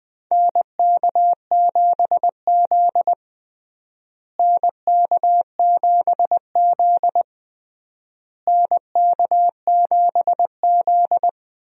Morse code, heard as a keyed tone, about 21 wpm, 700 Hz. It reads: NK7Z NK7Z NK7Z
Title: USA Call Signs - 4 Characters - Repeated 3x - 20wpm